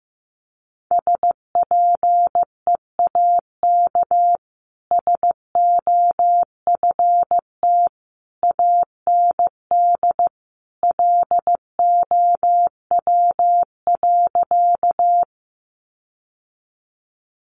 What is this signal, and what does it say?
Morse code, heard as a keyed tone, about 15 wpm, 700 Hz. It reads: SPEAK SOFT AND LOW.
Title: Sentences from Top 1500 Words 15wpm